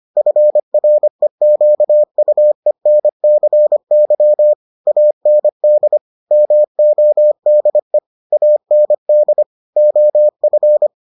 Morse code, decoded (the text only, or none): FREQUENCY AND MODE AND OF